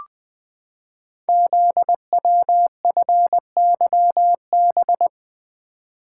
Morse code, heard as a keyed tone, about 20 wpm, 700 Hz. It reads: ZWFYB
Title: Code Groups - 5 Letters - 20wpm